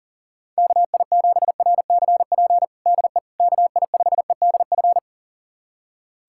Morse code, heard as a keyed tone, about 40 wpm, 700 Hz. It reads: KI7RCP DE KI5EDF
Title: USA Callsign DE Callsign 40wpm